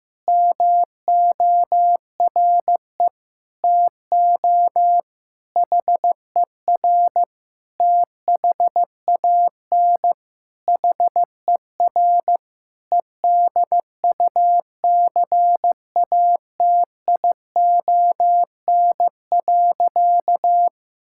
Morse code, heard as a keyed tone, about 15 wpm, 700 Hz. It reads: MORE TO HER THAN HER EDUCATION.
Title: Sentences from Top 1000 Words - Review Part 6 - 15wpm